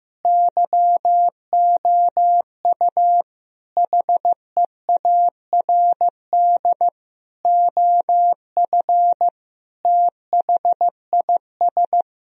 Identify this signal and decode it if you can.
Morse code, heard as a keyed tone, about 15 wpm, 700 Hz. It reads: YOU HEARD OF THIS
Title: Sentences from Top 1000 Words - Review Part 3 - 15wpm